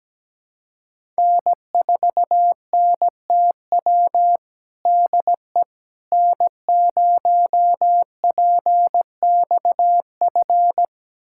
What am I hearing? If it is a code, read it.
Morse code, heard as a keyed tone, about 17 wpm, 700 Hz. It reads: N4NTW DE N0PXF
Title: USA Callsign DE Callsign - 5 Characters - 17wpm